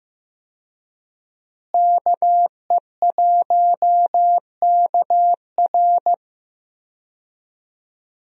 Morse code, heard as a keyed tone, about 16 wpm, 700 Hz. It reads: KE1KR
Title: USA Call Signs - 5 Characters - Repeated 2x - 15wpm